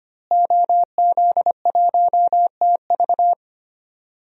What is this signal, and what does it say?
Morse code, heard as a keyed tone, about 25 wpm, 700 Hz. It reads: OZ1TV
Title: DXCC-100 Call Signs - 5 Characters - Repeated 3x - 25wpm